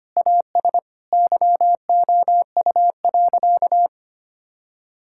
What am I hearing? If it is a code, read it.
Morse code, heard as a keyed tone, about 25 wpm, 700 Hz. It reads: AS YOU.
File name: practice / 25wpm / Sentences with 3-Word Phrases - 25wpm